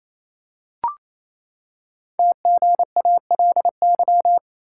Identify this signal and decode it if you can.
Morse code, heard as a keyed tone, about 27 wpm, 700 Hz. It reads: TGALY
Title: Code Groups - 5 Letters - 28wpm